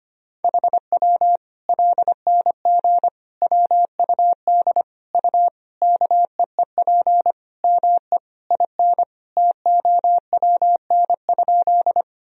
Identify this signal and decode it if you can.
Morse code, heard as a keyed tone, about 25 wpm, 700 Hz. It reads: HW LNG WUD U KEEP ME IN TOWN?
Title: Sentences from Top 500 Words and Abbreviations 25wpm